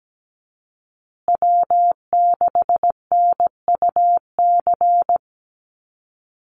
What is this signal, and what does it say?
Morse code, heard as a keyed tone, about 17 wpm, 700 Hz. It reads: W6NUC
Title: Call Signs Repeated 3x - 17wpm